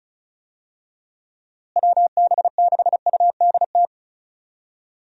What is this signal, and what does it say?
Morse code, heard as a keyed tone, about 34 wpm, 700 Hz. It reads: WB6UDT
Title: USA Call Signs - 6 Characters - Repeated 3x - 35wpm